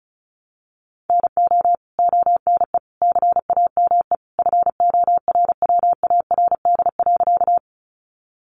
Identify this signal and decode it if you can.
Morse code, heard as a keyed tone, about 34 wpm, 700 Hz. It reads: NO ONE CAME FORWARD.
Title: Sentences from Top 1300 Words 35wpm